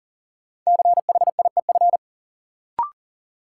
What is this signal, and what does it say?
Morse code, heard as a keyed tone, about 40 wpm, 700 Hz. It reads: CHIEF
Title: Sets of 5 Words Encore - New Word Drill - 40wpm